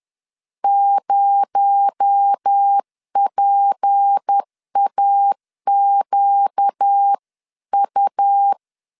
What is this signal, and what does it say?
Morse code, heard as a keyed tone, about 10 wpm, 800 Hz. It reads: 0PAQU
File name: CW / DET / 1 6 1500C S B 10wpm 4137